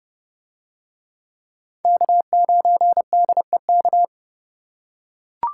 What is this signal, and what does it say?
Morse code, heard as a keyed tone, about 25 wpm, 700 Hz. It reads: K9DEK
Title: Speed Racing - USA 1x3 Call Signs 20wpm